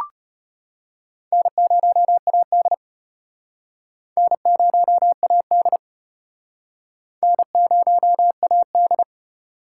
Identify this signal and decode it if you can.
Morse code, heard as a keyed tone, about 30 wpm, 700 Hz. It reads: N0AD N0AD N0AD
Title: Speed Racing - USA 1x2 Call Signs 25wpm